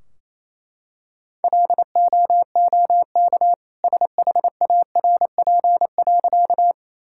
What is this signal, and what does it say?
Morse code, heard as a keyed tone, about 28 wpm, 700 Hz. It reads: LOOK SHARP.
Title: Sentences from Top 1600 Words 28wpm